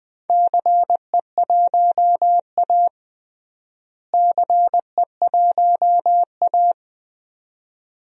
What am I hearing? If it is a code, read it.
Morse code, heard as a keyed tone, about 20 wpm, 700 Hz. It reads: CE1A CE1A
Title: DXCC-100 Call Signs - 4 Characters - Repeated 2x - 20wpm